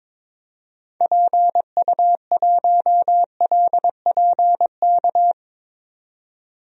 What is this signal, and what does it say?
Morse code, heard as a keyed tone, about 22 wpm, 700 Hz. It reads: PU1LPK
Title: DXCC-100 Call Signs - 6 Characters - 22wpm